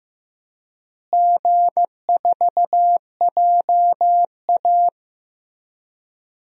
Morse code, heard as a keyed tone, about 15 wpm, 700 Hz. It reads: G4JA
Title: DXCC-100 Call Signs - 4 Characters - Repeated 3x - 15wpm